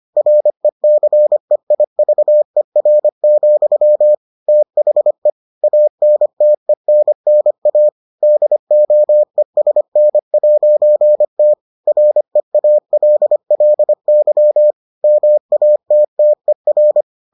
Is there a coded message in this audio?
Yes: Morse code from 0.2 to 17.0 s, heard as a keyed tone, about 25 wpm, 600 Hz. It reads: RECEIVER, THE ANTENNA DOESN'T REALLY MATTER